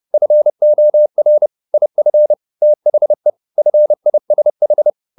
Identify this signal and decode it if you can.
Morse code, heard as a keyed tone, about 30 wpm, 600 Hz. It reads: FOR IF THE FISH
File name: the old man and the sea / the old man and the sea 1 30wpm